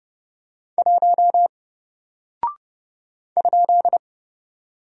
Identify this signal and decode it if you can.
Morse code, heard as a keyed tone, about 30 wpm, 700 Hz. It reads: 1 ?
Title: Single Character (Letter:Number:Punctuation:Prosign) Alt 30wpm